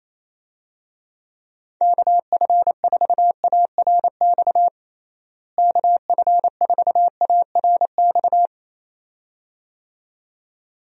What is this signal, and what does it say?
Morse code, heard as a keyed tone, about 28 wpm, 700 Hz. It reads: KF4ARX KF4ARX
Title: USA Call Signs - 6 Characters - Repeated 2x - 28wpm